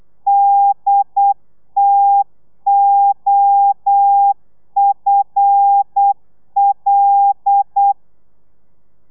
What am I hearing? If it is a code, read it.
Morse code, heard as a keyed tone, about 8 wpm, 800 Hz. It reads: DTOFL
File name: telegrafia / i2tfj / Lez5 35